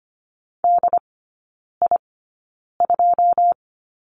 Morse code, heard as a keyed tone, about 25 wpm, spaced as for 9 wpm, 700 Hz. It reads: DI2
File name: Lesson 14e - Number 2 - Three Characters 25wpm